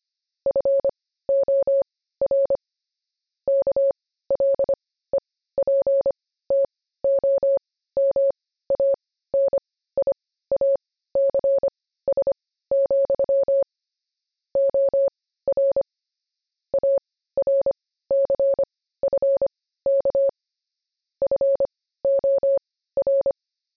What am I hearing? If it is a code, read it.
Morse code, heard as a keyed tone, about 25 wpm, spaced as for 15 wpm, 550 Hz. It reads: FOR KLEPTOMANIACS, OR ARCFK FOR
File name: jb recaptured / morse